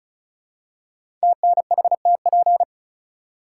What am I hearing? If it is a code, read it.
Morse code, heard as a keyed tone, about 35 wpm, 700 Hz. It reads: TNHTP
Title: Code Groups - 5 Letters - 35wpm